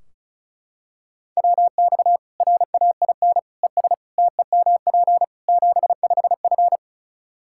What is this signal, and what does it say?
Morse code, heard as a keyed tone, about 36 wpm, 700 Hz. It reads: WX RAIN ES TEMP 75F
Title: Practice QSO - Exchange 2 - Info - 35wpm